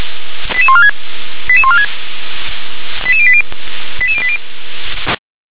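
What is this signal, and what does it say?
Five-tone selective call (ZVEI), heard as quick runs of five tones, tones of about 80 ms. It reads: 88146 88146 88080 88080
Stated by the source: ZVEI five-tone selective call received on an SDR.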